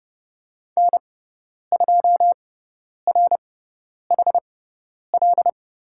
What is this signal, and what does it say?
Morse code, heard as a keyed tone, about 30 wpm, spaced as for 10 wpm, 700 Hz. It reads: N2RHL
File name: Lesson 16g - Letter c - Call Signs 30wpm